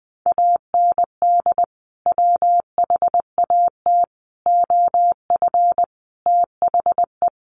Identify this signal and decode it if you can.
Morse code, heard as a keyed tone, about 19 wpm, 700 Hz. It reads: AND WHAT OF THE
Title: Sentences from Top 1500 Words 20wpm